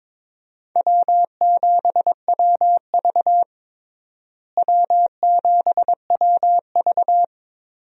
Morse code, heard as a keyed tone, about 22 wpm, 700 Hz. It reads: W7WV W7WV
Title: USA Call Signs - 4 Characters - Repeated 3x - 22wpm